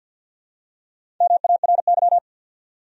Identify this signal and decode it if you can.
Morse code, heard as a keyed tone, about 53 wpm, 700 Hz. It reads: MARY
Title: USA Common Names - Max 4 Letters - 50wpm